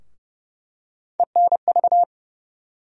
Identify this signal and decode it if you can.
Morse code, heard as a keyed tone, about 30 wpm, 700 Hz. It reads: ENV
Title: USA Call Sign Postfixes - 3 Letters 30wpm